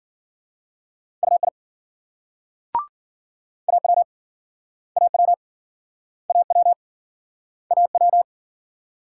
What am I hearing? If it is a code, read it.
Morse code, heard as a keyed tone, about 61 wpm, 700 Hz. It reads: UI AW AW AW AW